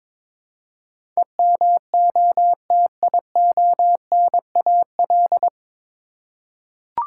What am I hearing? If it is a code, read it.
Morse code, heard as a keyed tone, about 23 wpm, 700 Hz. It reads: EMOTIONAL
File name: 9 Letter Words - First 3 Letters Given - 22wpm